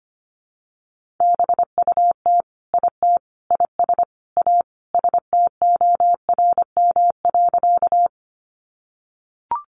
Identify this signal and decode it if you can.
Morse code, heard as a keyed tone, about 25 wpm, 700 Hz. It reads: BUT IT IS A STORM.
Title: Sentences from Top 1500 Words 25wpm